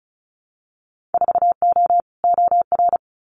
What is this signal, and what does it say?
Morse code, heard as a keyed tone, about 36 wpm, 700 Hz. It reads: 4O OR
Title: Winter Field Day Exchanges Encore 35wpm